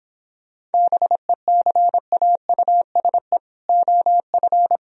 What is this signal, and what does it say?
Morse code, heard as a keyed tone, about 29 wpm, 700 Hz. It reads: BECAUSE OF